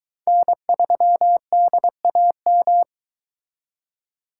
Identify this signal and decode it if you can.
Morse code, heard as a keyed tone, about 22 wpm, 700 Hz. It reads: N3DAM